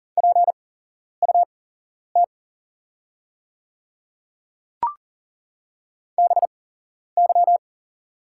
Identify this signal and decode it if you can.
Morse code, heard as a keyed tone, about 40 wpm, spaced as for 11 wpm, 700 Hz. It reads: PUT BY
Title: 8x Character Spacing - Top 200 Words - 40wpm